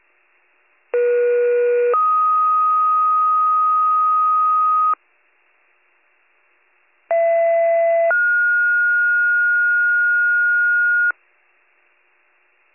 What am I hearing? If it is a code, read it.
Two-tone page, heard as a one-second tone, then a three-second tone, about 480 then 1250 Hz, about 670 then 1430 Hz. It is heard from 0.9 to 11.1 s.